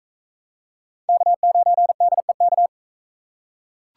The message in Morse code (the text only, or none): K9DEK